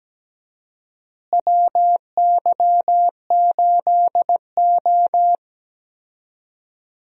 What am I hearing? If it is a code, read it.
Morse code, heard as a keyed tone, about 17 wpm, 700 Hz. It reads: WY8O
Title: USA Call Signs - 4 Characters - Repeated 3x - 17wpm